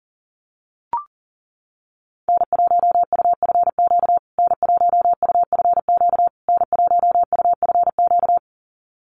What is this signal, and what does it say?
Morse code, heard as a keyed tone, about 40 wpm, 700 Hz. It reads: N1UFQ N1UFQ N1UFQ
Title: USA Call Signs Repeated 3x 40wpm